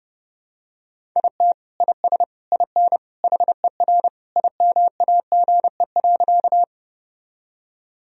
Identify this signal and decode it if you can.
Morse code, heard as a keyed tone, about 30 wpm, 700 Hz. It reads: IT IS IN HER IMAGE.
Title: Sentences from Top 1000 Words - Review Part 4 - 30wpm